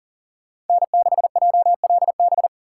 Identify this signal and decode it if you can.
Morse code, heard as a keyed tone, about 51 wpm, 700 Hz. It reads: N6JLB